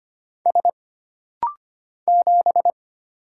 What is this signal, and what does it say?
Morse code, heard as a keyed tone, about 25 wpm, 700 Hz. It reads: S 7
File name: Single Character (Letter:Number:Punctuation:Prosign) - Rapid-Fire - 25wpm